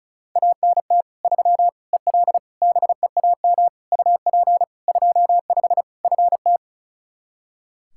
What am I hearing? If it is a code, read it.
Morse code, heard as a keyed tone, about 36 wpm, 700 Hz. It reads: ANT 3 EL BEAM UP 25 FT
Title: K1USN SST QSO Practice 35wpm